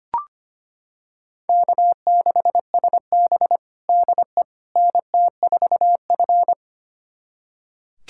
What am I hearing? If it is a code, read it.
Morse code, heard as a keyed tone, about 25 wpm, 700 Hz. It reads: K6SB DE NT4F
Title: USA Callsign DE Callsign - 4 Characters - 25wpm